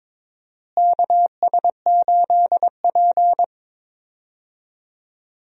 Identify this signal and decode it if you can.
Morse code, heard as a keyed tone, about 22 wpm, 700 Hz. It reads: KS8P
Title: USA Call Signs - 4 Characters - Repeated 3x - 22wpm